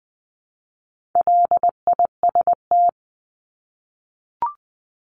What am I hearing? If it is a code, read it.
Morse code, heard as a keyed tone, about 20 wpm, 700 Hz. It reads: LIST
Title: 260 Lesson 42e - Words 20-20wpm